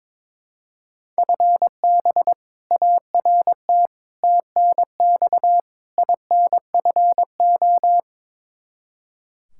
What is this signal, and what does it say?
Morse code, heard as a keyed tone, about 22 wpm, 700 Hz. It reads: FB ART TNX INFO
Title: K1USN SST QSO Practice 22wpm